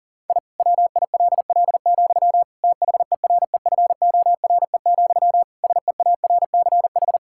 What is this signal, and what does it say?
Morse code, heard as a keyed tone, about 40 wpm, 700 Hz. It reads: I WILL, THEREFORE, SEARCH
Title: Sentences from Top 1000 Words - Review Part 5 - 40wpm